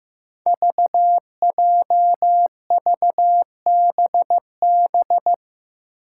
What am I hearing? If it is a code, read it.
Morse code, heard as a keyed tone, about 16 wpm, 700 Hz. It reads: VJVBB
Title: Five Challenging Letter-Number Combinations 15wpm